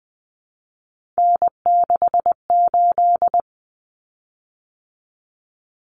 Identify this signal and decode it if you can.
Morse code, heard as a keyed tone, about 20 wpm, 700 Hz. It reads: N68
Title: Common Cut Numbers - 3-Digit 20wpm